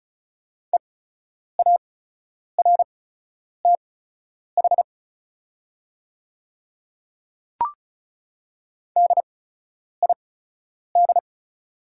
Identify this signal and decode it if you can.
Morse code, heard as a keyed tone, about 36 wpm, spaced as for 10 wpm, 700 Hz. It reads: EARTH DID
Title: 8x Character Spacing - Top 300 Words - 35wpm